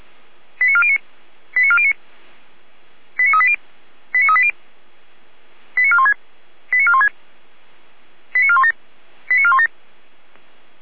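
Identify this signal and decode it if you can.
Five-tone selective call (ZVEI), heard as quick runs of five tones, tones of about 70 ms. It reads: 89409 89409 89380 89380 89416 89416 89417 89417
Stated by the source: ZVEI five-tone call received on an SDR.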